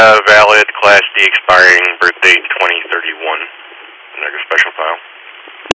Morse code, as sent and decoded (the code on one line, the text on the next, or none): none